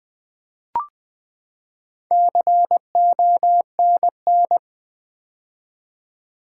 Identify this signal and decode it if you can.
Morse code, heard as a keyed tone, about 20 wpm, 700 Hz. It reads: CONN
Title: Build A Word - 4 Letters At A Time - 10-Letter Words - 20wpm